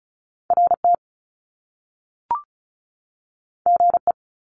Morse code, heard as a keyed tone, about 35 wpm, 700 Hz. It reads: RT GE
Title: Top Two Letter Combinations in Words 35wpm